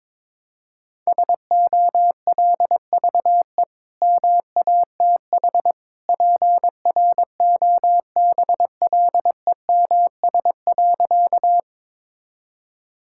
Morse code, heard as a keyed tone, about 22 wpm, 700 Hz. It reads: SOLVE MATH PROBLEMS.
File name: practice / 22wpm / Easy Sentences with Hints - 22wpm